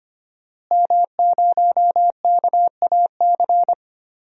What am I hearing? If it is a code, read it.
Morse code, heard as a keyed tone, about 25 wpm, 700 Hz. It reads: M0KAC
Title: DXCC-100 Call Signs - 5 Characters - Repeated 3x - 25wpm